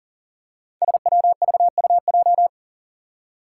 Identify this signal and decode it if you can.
Morse code, heard as a keyed tone, about 40 wpm, 700 Hz. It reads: SWVUJ
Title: Five Challenging Letter-Number Combinations 40wpm